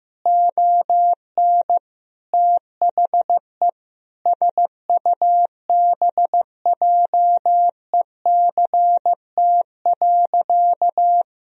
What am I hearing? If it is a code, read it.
Morse code, heard as a keyed tone, about 15 wpm, 700 Hz. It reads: ON THE SUBJECT.
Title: Sentences from Top 1000 Words - Review Part 3 - 15wpm